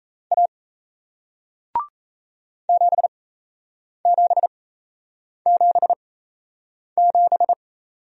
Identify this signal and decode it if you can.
Morse code, heard as a keyed tone, about 38 wpm, spaced as for 8 wpm, 700 Hz. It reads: A 7777